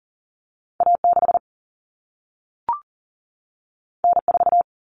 Morse code, heard as a keyed tone, about 40 wpm, 700 Hz. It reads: A6 N4
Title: Common Cut Numbers - 2-Digit 40wpm